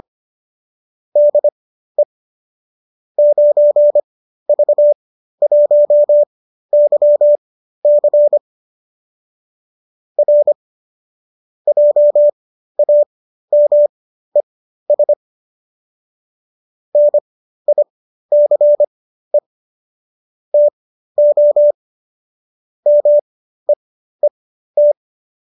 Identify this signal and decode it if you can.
Morse code, heard as a keyed tone, about 25 wpm, spaced as for 13 wpm, 590 Hz. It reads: DE 9V1YC R JAMES NICE TO MEET
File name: qso207 13